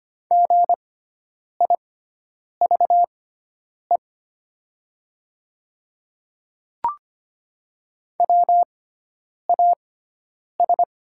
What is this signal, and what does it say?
Morse code, heard as a keyed tone, about 25 wpm, spaced as for 9 wpm, 700 Hz. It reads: GIVE WAS
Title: 6x Character Spacing - Top 200 Words - 25wpm